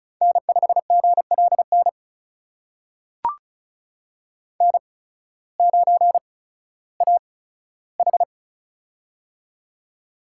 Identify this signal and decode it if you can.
Morse code, heard as a keyed tone, about 34 wpm, spaced as for 16 wpm, 700 Hz. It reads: N5GLN N9AH
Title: USA Call Signs - 8x Character Spacing - 35wpm